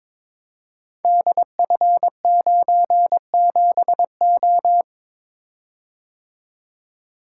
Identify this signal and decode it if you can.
Morse code, heard as a keyed tone, about 22 wpm, 700 Hz. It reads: DF97O
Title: Code Groups - 5 Letter:Numbers - 22wpm